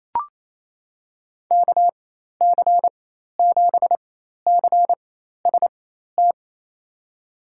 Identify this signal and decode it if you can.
Morse code, heard as a keyed tone, about 28 wpm, spaced as for 13 wpm, 700 Hz. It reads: KC7CST
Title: USA Call Signs - 6 Characters - 4x Character Spacing - 28wpm